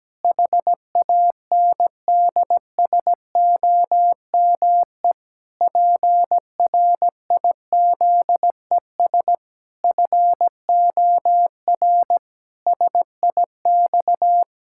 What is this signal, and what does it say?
Morse code, heard as a keyed tone, about 17 wpm, 700 Hz. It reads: HANDSOME PRIZES FOR SIX
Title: Pangram Challenge v2 - 17wpm